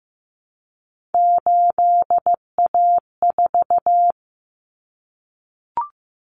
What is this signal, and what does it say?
Morse code, heard as a keyed tone, about 15 wpm, 700 Hz. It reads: 8A4
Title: Common Cut Numbers - 3-Digit 15wpm